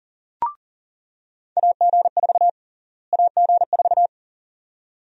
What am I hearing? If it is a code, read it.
Morse code, heard as a keyed tone, about 40 wpm, 700 Hz. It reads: AG4 AG4
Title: USA Call Sign Prefixes - Repeated 2x - 40wpm